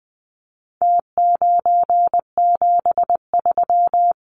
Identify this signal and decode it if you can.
Morse code, heard as a keyed tone, about 20 wpm, 700 Hz. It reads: T973
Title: Numbers - Contest Style - 4-Digit 20wpm